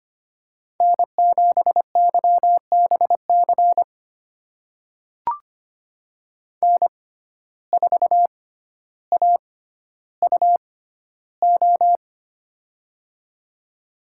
Morse code, heard as a keyed tone, about 25 wpm, spaced as for 13 wpm, 700 Hz. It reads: N7YBC N4AUO
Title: USA Call Signs - 5 Characters - 6x Character Spacing - 25wpm